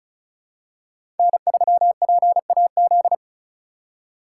Morse code, heard as a keyed tone, about 36 wpm, 700 Hz. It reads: N3PAZ
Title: USA Call Signs - 5 Characters - 3x Character Spacing - 35wpm